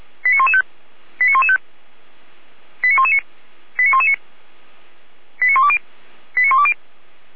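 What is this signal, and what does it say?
Five-tone selective call (ZVEI), heard as quick runs of five tones, tones of about 70 ms. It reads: 89115 89115 89119 89119 89120 89120
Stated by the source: ZVEI five-tone call received by SDR, logged on Feuerwehr Orlamünde audio server.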